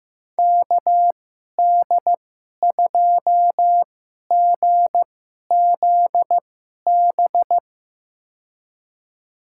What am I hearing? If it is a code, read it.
Morse code, heard as a keyed tone, about 15 wpm, spaced as for 11 wpm, 700 Hz. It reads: KD2GZB
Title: USA Call Signs - 6 Characters - 2x Character Spacing - 15wpm